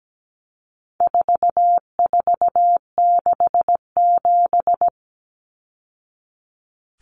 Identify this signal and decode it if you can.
Morse code, heard as a keyed tone, about 17 wpm, 700 Hz. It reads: 4467
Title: Numbers - 4-Digit 17wpm